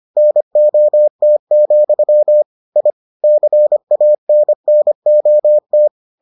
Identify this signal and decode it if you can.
Morse code, heard as a keyed tone, about 25 wpm, 600 Hz. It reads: NOT, I CANNOT